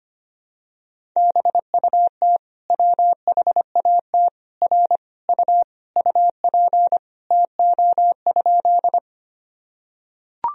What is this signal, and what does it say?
Morse code, heard as a keyed tone, about 25 wpm, 700 Hz. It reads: BUT WHAT R U UP TO?
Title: Sentences from Top 100 Words and Abbreviations 25wpm